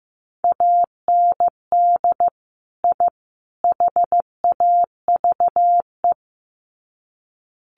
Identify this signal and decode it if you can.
Morse code, heard as a keyed tone, about 15 wpm, 700 Hz. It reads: AND I HAVE
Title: Sets of 3 Words 15wpm